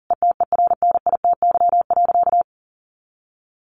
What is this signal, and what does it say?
Morse code, heard as a keyed tone, about 40 wpm, 700 Hz. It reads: ETERNITY.
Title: A Sound Of Thunder 40wpm